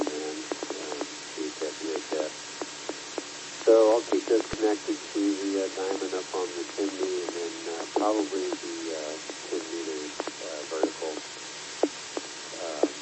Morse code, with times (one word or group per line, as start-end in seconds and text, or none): none